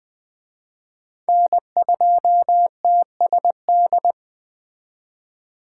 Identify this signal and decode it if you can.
Morse code, heard as a keyed tone, about 20 wpm, 700 Hz. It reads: N2TSD
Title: Speed Racing - USA 1x3 Call Signs 17wpm